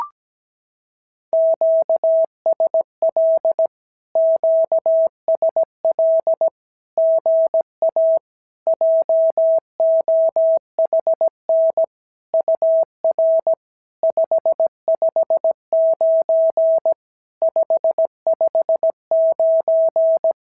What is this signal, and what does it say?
Morse code, heard as a keyed tone, about 17 wpm, 650 Hz. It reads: QSL QSL GA JOHN UR 559 559
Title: SOTA QSO Practice 17wpm